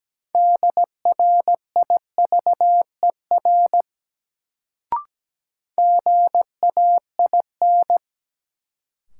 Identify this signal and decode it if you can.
Morse code, heard as a keyed tone, about 17 wpm, 700 Hz. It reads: DRIVER GAIN
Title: Sentences from Top 1000 Words - New Word Drill - 17wpm